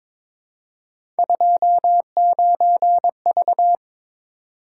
Morse code, heard as a keyed tone, about 22 wpm, 700 Hz. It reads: 29V
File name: practice / 22wpm / Three Letter-Number Combinations - 22wpm